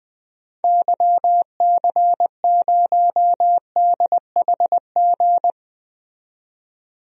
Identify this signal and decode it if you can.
Morse code, heard as a keyed tone, about 20 wpm, 700 Hz. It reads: YC0DHG
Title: DXCC-100 Call Signs - 6 Characters - 20wpm